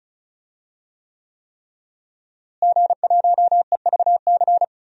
Morse code, heard as a keyed tone, about 34 wpm, 700 Hz. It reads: G1EVC
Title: DXCC-100 Call Signs - 5 Characters - Repeated 3x - 35wpm